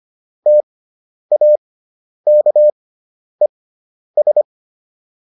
Morse code, heard as a keyed tone, about 25 wpm, spaced as for 10 wpm, 600 Hz. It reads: TAKES